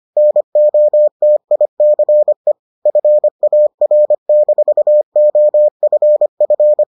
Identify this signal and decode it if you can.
Morse code, heard as a keyed tone, about 25 wpm, 600 Hz. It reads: NOTICE FAR-OFF